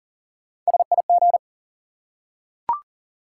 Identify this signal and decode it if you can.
Morse code, heard as a keyed tone, about 40 wpm, 700 Hz. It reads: SIG
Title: Top 180 QSO Elements 40wpm